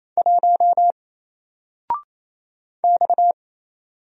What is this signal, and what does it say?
Morse code, heard as a keyed tone, about 28 wpm, 700 Hz. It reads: 1 X
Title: Single Letter-Number 28wpm